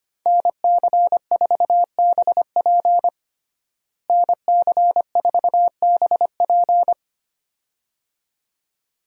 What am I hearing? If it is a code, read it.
Morse code, heard as a keyed tone, about 25 wpm, 700 Hz. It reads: NC4BP NC4BP
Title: USA Call Signs - 5 Characters - Repeated 2x - 25wpm